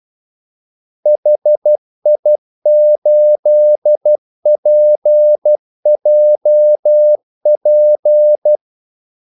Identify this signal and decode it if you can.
Morse code, heard as a keyed tone, about 12 wpm, 600 Hz. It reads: HI8PJP
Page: 2 ekstra ordmellemrum mellem kaldesignalerne